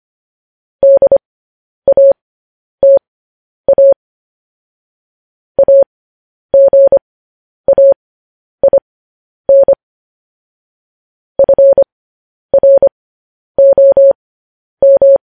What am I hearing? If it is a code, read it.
Morse code, heard as a keyed tone, about 25 wpm, spaced as for 10 wpm, 550 Hz. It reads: DATA AGAIN FROM